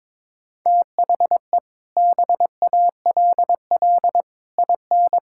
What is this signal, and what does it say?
Morse code, heard as a keyed tone, about 22 wpm, 700 Hz. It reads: THE BALL IN
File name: practice / 22wpm / Sets of 8 Words - 22wpm